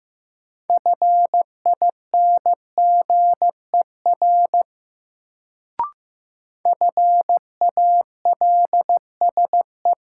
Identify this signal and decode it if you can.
Morse code, heard as a keyed tone, about 16 wpm, 700 Hz. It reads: FINGER FALSE
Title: Sentences from Top 1600 Words - New Word Drill - 15wpm